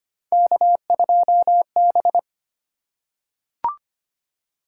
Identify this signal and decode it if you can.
Morse code, heard as a keyed tone, about 25 wpm, 700 Hz. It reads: K2B
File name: Three Letter-Number Combinations - 25wpm